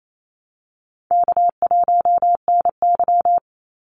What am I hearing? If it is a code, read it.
Morse code, heard as a keyed tone, about 28 wpm, 700 Hz. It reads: K1NY
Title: Easy Call Signs Repeated 3x - 28wpm